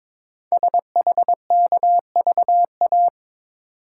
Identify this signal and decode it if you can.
Morse code, heard as a keyed tone, about 23 wpm, 690 Hz. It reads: SHKVA
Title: Code Groups - 5 Letters - 22wpm